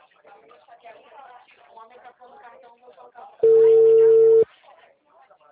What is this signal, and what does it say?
Ringback tone (China, 450 Hz), with speech.